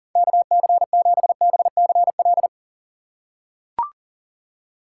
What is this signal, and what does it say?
Morse code, heard as a keyed tone, about 40 wpm, 700 Hz. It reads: KC7BCL
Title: USA Call Signs - 6 Characters - 3x Character Spacing - 40wpm